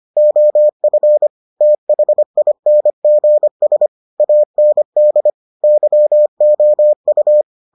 Morse code, heard as a keyed tone, about 25 wpm, 600 Hz. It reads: OF THINGS AND YOU